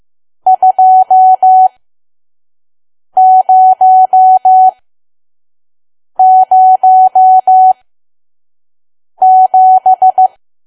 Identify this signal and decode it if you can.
Morse code, heard as a keyed tone, about 15 wpm, spaced as for 5 wpm, 750 Hz. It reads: 2007
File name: Archive / 5 WPM files / PM